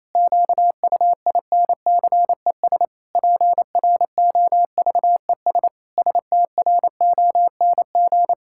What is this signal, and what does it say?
Morse code, heard as a keyed tone, about 27 wpm, 700 Hz. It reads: QUINCES PROVES STRONG
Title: Pangram Challenge v2 - 28wpm